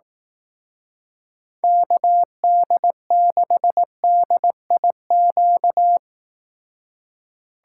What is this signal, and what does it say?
Morse code, heard as a keyed tone, about 18 wpm, 700 Hz. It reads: KD6DIQ
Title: Speed Racing - USA Call Signs 15wpm